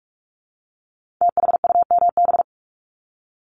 Morse code, heard as a keyed tone, about 46 wpm, 700 Hz. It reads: THUMB
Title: Multi-letter Phonemes Repeated 3x 45wpm